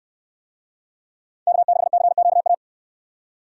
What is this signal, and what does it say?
Morse code, heard as a keyed tone, about 64 wpm, 700 Hz. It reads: K6CQA